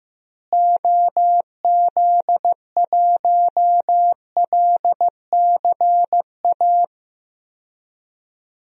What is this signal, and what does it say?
Morse code, heard as a keyed tone, about 15 wpm, 700 Hz. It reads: OZ1LCA
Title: DXCC-100 Call Signs - 6 Characters - Repeated 2x - 15wpm